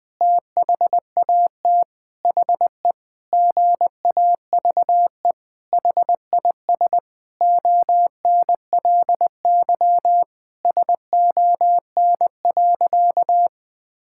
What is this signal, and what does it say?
Morse code, heard as a keyed tone, about 20 wpm, 700 Hz. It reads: THAT HE GAVE HIS ONLY SON.
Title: Sentences with 8-Word Phrases - 20wpm